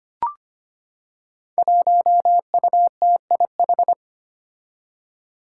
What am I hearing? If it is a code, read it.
Morse code, heard as a keyed tone, about 25 wpm, 700 Hz. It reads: 1UTIH
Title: Five Challenging Letter-Number Combinations 25wpm